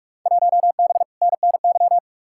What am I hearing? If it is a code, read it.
Morse code, heard as a keyed tone, about 46 wpm, 700 Hz. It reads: 1B NNY
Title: Field Day Exchanges Encore - 45wpm